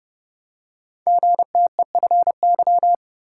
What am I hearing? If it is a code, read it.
Morse code, heard as a keyed tone, about 30 wpm, 700 Hz. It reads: GTEFY
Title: Code Groups - 5 Letter:Numbers - 30wpm